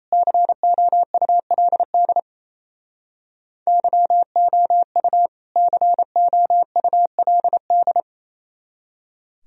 Morse code, heard as a keyed tone, about 37 wpm, 700 Hz. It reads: COULD YOU COULD